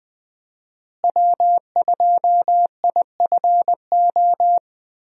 Morse code, heard as a keyed tone, about 20 wpm, 700 Hz. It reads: W2IFO
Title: USA Call Signs - 5 Characters - Repeated 2x - 20wpm